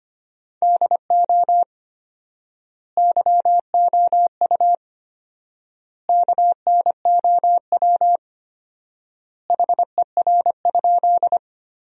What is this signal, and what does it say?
Morse code, heard as a keyed tone, about 25 wpm, 700 Hz. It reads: DO YOU KNOW HER?
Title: 4x Word-Spacing - Sentences from Top 100 Words - 25wpm